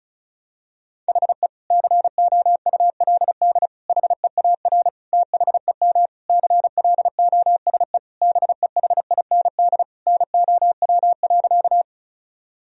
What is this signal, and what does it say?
Morse code, heard as a keyed tone, about 34 wpm, 700 Hz. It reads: HE COULD HEAR THEM CLOSE BEHIND NOW.
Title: Sentences from Top 1000 Words - Review Part 1 - 35wpm